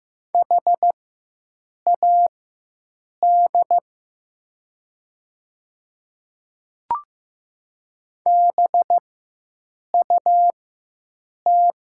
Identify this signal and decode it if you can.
Morse code, heard as a keyed tone, about 15 wpm, spaced as for 7 wpm, 700 Hz. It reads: HAD BUT
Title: 4x Character Spacing - Top 100 Words - 15wpm